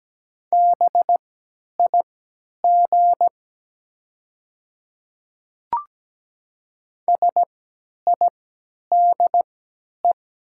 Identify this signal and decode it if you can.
Morse code, heard as a keyed tone, about 17 wpm, spaced as for 10 wpm, 700 Hz. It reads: BIG SIDE
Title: 3x Character Spacing - Top 300 Words - 17wpm